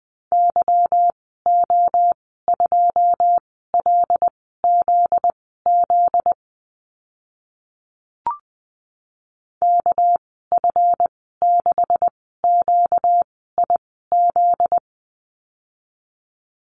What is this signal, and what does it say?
Morse code, heard as a keyed tone, about 20 wpm, spaced as for 15 wpm, 700 Hz. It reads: YO2LZZ KF6QIZ
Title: 245 Lesson 39g - Letter z - Call Signs 20wpm